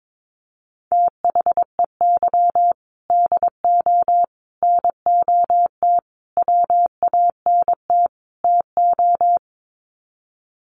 Morse code, heard as a keyed tone, about 22 wpm, 700 Hz. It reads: THEY DO NOT WANT TO
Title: Sets of 5 Words 22wpm